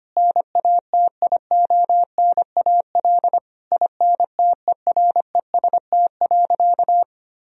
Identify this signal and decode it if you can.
Morse code, heard as a keyed tone, about 25 wpm, 700 Hz. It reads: NATIONAL INTEREST.
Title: Sentences from Top 1000 Words - Review Part 4 - 25wpm